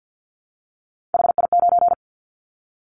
Morse code, heard as a keyed tone, about 50 wpm, 700 Hz. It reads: HI9
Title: Callsign Prefixes for Top 100 DXCC - 50wpm